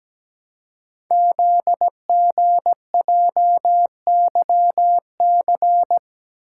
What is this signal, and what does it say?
Morse code, heard as a keyed tone, about 17 wpm, 700 Hz. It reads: ZGJYC
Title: Code Groups - 5 Letter:Numbers - 17wpm